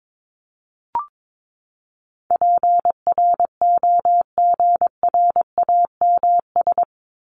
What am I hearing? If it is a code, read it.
Morse code, heard as a keyed tone, about 22 wpm, 700 Hz. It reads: PROGRAMS
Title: ES:S Suffix - Top 100 Words with Base Words 22wpm